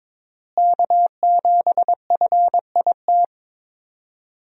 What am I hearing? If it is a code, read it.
Morse code, heard as a keyed tone, about 21 wpm, 700 Hz. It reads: K7FIT
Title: USA Call Signs - 5 Characters - Repeated 3x - 22wpm